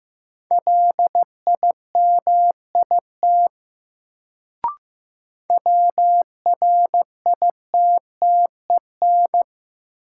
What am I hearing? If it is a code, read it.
Morse code, heard as a keyed tone, about 15 wpm, 700 Hz. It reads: LIMIT WRITTEN
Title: Sentences from Top 1500 Words - New Word Drill Alt - 15wpm